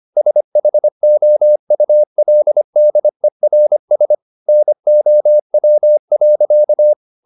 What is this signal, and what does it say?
Morse code, heard as a keyed tone, about 25 wpm, 600 Hz. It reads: SHOULDERS NOW.
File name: the old man and the sea part 1 25wpm